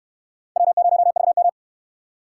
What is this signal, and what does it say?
Morse code, heard as a keyed tone, about 65 wpm, 700 Hz. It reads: W0VM